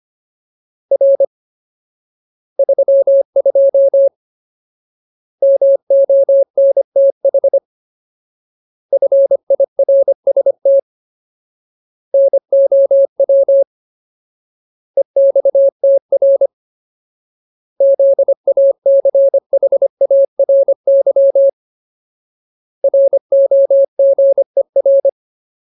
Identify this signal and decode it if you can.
Morse code, heard as a keyed tone, about 25 wpm, 550 Hz. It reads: R 32 MONTH FIRST NOW EXTR ZACHARY ROGER
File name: Ultimate Word List / 25WPM Ultimate